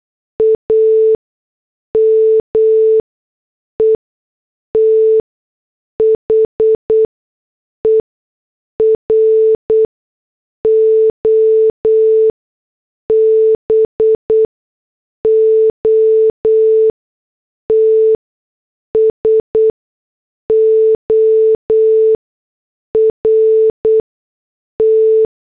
Morse code, heard as a keyed tone, about 8 wpm, spaced as for 6 wpm, 440 Hz. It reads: AME THE ROBOT SORT
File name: morse / headlines